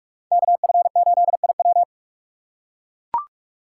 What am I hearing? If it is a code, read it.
Morse code, heard as a keyed tone, about 41 wpm, 700 Hz. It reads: KF8IW